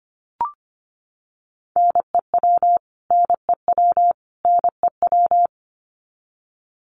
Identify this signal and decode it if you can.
Morse code, heard as a keyed tone, about 25 wpm, 700 Hz. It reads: NEW NEW NEW
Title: Top 500 Words Repeated 25wpm